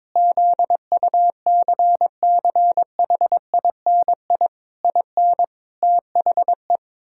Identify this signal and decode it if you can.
Morse code, heard as a keyed tone, about 22 wpm, 700 Hz. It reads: ZUCCHINI IN THE
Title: Pangram Challenge v2 - 22wpm